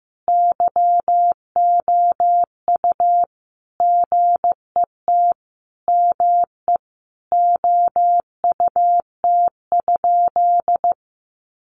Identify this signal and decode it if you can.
Morse code, heard as a keyed tone, about 15 wpm, 700 Hz. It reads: YOU GET ME OUT?
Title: Sentences from Top 200 Words 15wpm